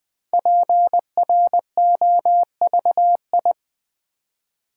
Morse code, heard as a keyed tone, about 20 wpm, 700 Hz. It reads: PROVI
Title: Build A Word - 5 Letters At A Time - 10-Letter Words - 20wpm